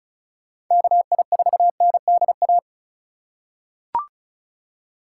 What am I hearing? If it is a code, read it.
Morse code, heard as a keyed tone, about 36 wpm, 700 Hz. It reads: KI4NDA
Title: USA Call Signs - 6 Characters - 3x Character Spacing - 35wpm